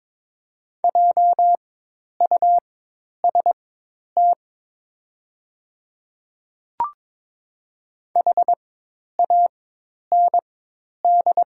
Morse code, heard as a keyed tone, about 22 wpm, spaced as for 10 wpm, 700 Hz. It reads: JUST HAND
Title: 4x Character Spacing - Top 200 Words - 22wpm